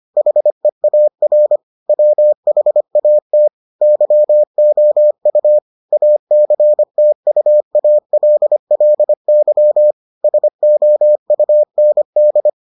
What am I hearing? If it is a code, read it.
Morse code, heard as a keyed tone, about 25 wpm, 600 Hz. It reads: HEAR WHAT YOU ACTUALLY SOUND